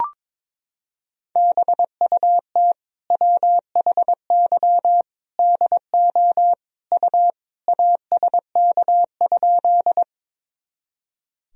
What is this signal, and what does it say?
Morse code, heard as a keyed tone, about 22 wpm, 700 Hz. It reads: BUT WHY DO U ASK?
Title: Sentences from Top 200 Words and Abbreviations 22wpm